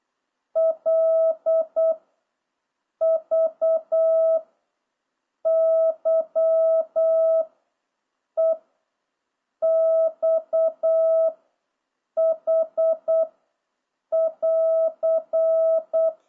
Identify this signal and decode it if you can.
Morse code, heard as a keyed tone, about 8 wpm, 660 Hz. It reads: L V Y E X H <AR>